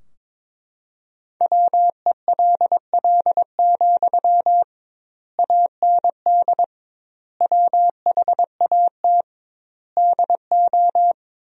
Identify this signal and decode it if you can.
Morse code, heard as a keyed tone, about 22 wpm, 700 Hz. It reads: WELL, AND WHAT DO
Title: 2x Word-Spacing - Sentences from Top 200 Words - 22wpm